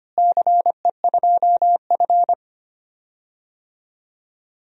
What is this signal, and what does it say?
Morse code, heard as a keyed tone, about 25 wpm, 700 Hz. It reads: CE2F
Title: DXCC-100 Call Signs - 4 Characters - Repeated 3x - 25wpm